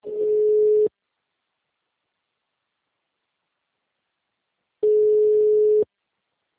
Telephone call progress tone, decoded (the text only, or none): ringback tone